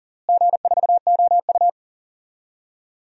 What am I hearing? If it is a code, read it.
Morse code, heard as a keyed tone, about 40 wpm, 700 Hz. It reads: G4OU
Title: DXCC-100 Call Signs - 4 Characters - Repeated 3x - 40wpm